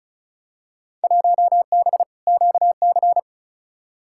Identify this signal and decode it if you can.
Morse code, heard as a keyed tone, about 34 wpm, 700 Hz. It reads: 1B QC
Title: Field Day Exchanges - 35wpm